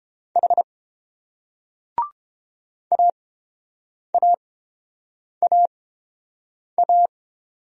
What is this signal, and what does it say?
Morse code, heard as a keyed tone, about 31 wpm, spaced as for 7 wpm, 700 Hz. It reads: H AAAA